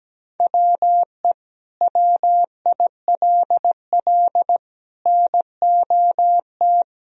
Morse code, heard as a keyed tone, about 17 wpm, 700 Hz. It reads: WE WILL NOT
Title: Sentences from Top 1000 Words - Review Part 2 - 17wpm